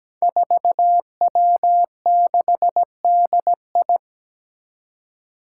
Morse code, heard as a keyed tone, about 17 wpm, 700 Hz. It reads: 4W6DI
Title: Five Challenging Letter-Number Combinations 17wpm